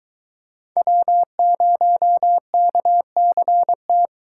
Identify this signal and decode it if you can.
Morse code, heard as a keyed tone, about 22 wpm, 700 Hz. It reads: W0KCT